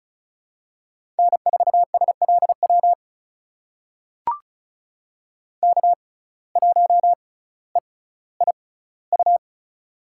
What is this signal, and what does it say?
Morse code, heard as a keyed tone, about 36 wpm, spaced as for 18 wpm, 700 Hz. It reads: N4SLW K1EIU
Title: USA Call Signs - 5 Characters - 6x Character Spacing - 35wpm